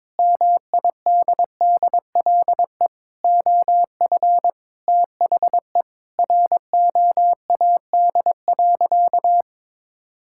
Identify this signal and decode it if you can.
Morse code, heard as a keyed tone, about 22 wpm, 700 Hz. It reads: MIDDLE OF THE ROAD.
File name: practice / 22wpm / Sentences with 3-Word Phrases - 22wpm